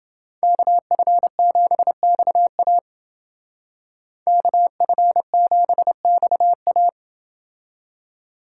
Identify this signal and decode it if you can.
Morse code, heard as a keyed tone, about 28 wpm, 700 Hz. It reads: KF7XA KF7XA